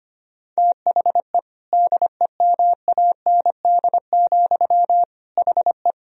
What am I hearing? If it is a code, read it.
Morse code, heard as a keyed tone, about 25 wpm, 700 Hz. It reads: THE DEMAND, HE
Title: Sentences from Top 1000 Words - Review Part 4 - 25wpm